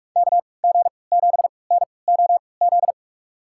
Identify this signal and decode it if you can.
Morse code, heard as a keyed tone, about 46 wpm, spaced as for 25 wpm, 700 Hz. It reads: KG7NOZ
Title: USA Call Signs - 6 Characters - 3x Character Spacing - 45wpm